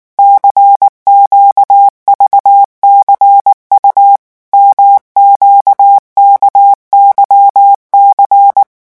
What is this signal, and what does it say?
Morse code, heard as a keyed tone, about 19 wpm, 800 Hz. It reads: CQVCU MQKYC
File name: morse mp3 / revision2-vitesse 18 mots